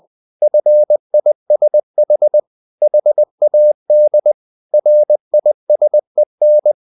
Morse code, heard as a keyed tone, about 20 wpm, 600 Hz. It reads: FISH HAD RISEN
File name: the old man and the sea part 2 20wpm